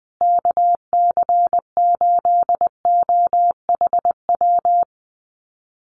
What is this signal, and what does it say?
Morse code, heard as a keyed tone, about 20 wpm, 700 Hz. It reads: KC8OHW
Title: USA Call Signs Repeated 3x 20wpm